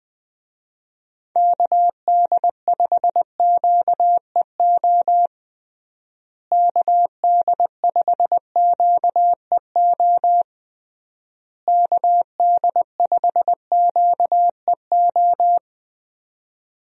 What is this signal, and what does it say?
Morse code, heard as a keyed tone, about 20 wpm, 700 Hz. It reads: KD5QEO KD5QEO KD5QEO
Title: USA Call Signs - 6 Characters - Repeated 3x - 20wpm